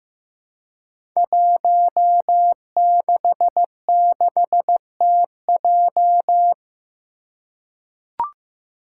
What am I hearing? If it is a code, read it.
Morse code, heard as a keyed tone, about 15 wpm, 700 Hz. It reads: 166TJ
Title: Five Challenging Letter-Number Combinations 15wpm